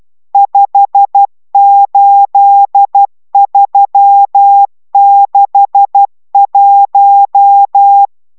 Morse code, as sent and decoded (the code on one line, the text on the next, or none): ..... ---.. ...-- -.... .----
58361